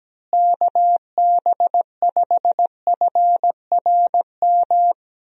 Morse code, heard as a keyed tone, about 17 wpm, 700 Hz. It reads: KB5FRM
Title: USA Call Signs - 6 Characters - Repeated 2x - 17wpm